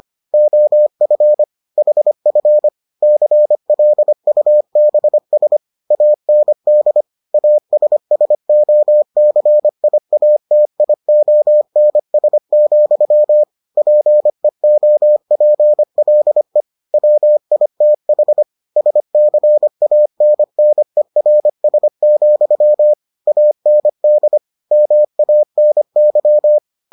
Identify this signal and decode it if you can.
Morse code, heard as a keyed tone, about 25 wpm, 600 Hz. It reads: OF HF CLUBS AND ASSOCIATIONS, PEOPLE WITH SCANNERS, AND MANY